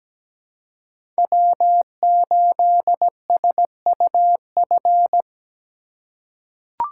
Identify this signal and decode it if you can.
Morse code, heard as a keyed tone, about 17 wpm, 700 Hz. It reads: W8SUF
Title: USA Call Signs - 3x Character Spacing - 17wpm